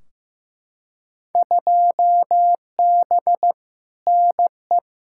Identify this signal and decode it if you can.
Morse code, heard as a keyed tone, about 15 wpm, 700 Hz. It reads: 2B NE
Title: Field Day Exchanges Encore - 15wpm